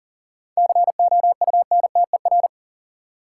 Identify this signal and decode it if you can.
Morse code, heard as a keyed tone, about 40 wpm, 700 Hz. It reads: COUNTER
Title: Seven Letter Words - 40wpm